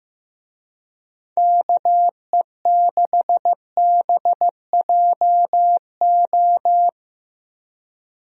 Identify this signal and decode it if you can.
Morse code, heard as a keyed tone, about 14 wpm, 700 Hz. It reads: KE6BJO
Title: USA Call Signs - 6 Characters - Repeated 3x - 15wpm